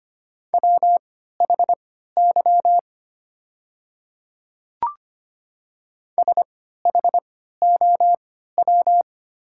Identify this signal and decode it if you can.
Morse code, heard as a keyed tone, about 25 wpm, spaced as for 14 wpm, 700 Hz. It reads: WHY SHOW
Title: 3x Character Spacing - Top 200 Words - 25wpm